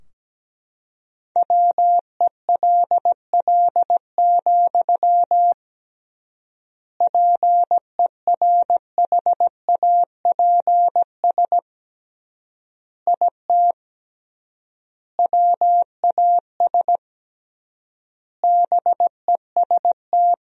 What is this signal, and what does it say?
Morse code, heard as a keyed tone, about 17 wpm, 700 Hz. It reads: WELL, PERHAPS IT WAS BEST
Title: 3x Word-Spacing - Sentences from Top 500 Words - 17wpm